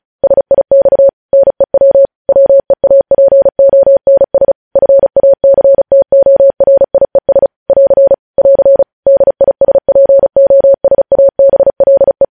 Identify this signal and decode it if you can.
Morse code, heard as a keyed tone, about 36 wpm, 550 Hz. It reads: SIX NEW WEAPONS FACTORIES <AR> <AR> DISPOSABLE
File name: podcast / News headlines-1x-35x35